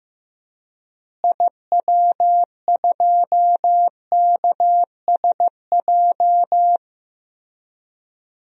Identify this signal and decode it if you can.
Morse code, heard as a keyed tone, about 15 wpm, 700 Hz. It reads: IW2KSJ
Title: DXCC-100 Call Signs - 6 Characters - Repeated 2x - 15wpm